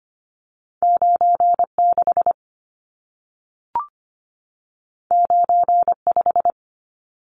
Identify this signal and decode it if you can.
Morse code, heard as a keyed tone, about 25 wpm, 700 Hz. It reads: 96 95
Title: Numbers - 2-Digit 25wpm